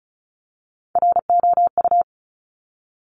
Morse code, heard as a keyed tone, about 36 wpm, 700 Hz. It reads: ROU
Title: Top Three Letter Combinations in Words 35wpm